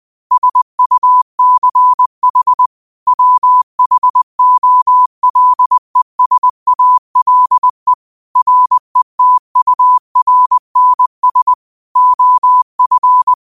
Morse code, heard as a keyed tone, about 20 wpm, 1000 Hz. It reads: SUCH WHOLESALE RETURNS OF
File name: qotdfiles / 20wpm